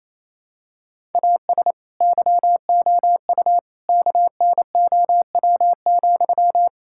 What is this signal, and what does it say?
Morse code, heard as a keyed tone, about 28 wpm, 700 Hz. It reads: AS YOU KNOW,
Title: Sentences from Top 1000 Words - Review Part 5 - 28wpm